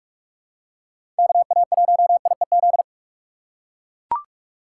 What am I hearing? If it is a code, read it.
Morse code, heard as a keyed tone, about 43 wpm, 690 Hz. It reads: KA1IEZ
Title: USA Call Signs - 6 Characters - 2x Character Spacing - 45wpm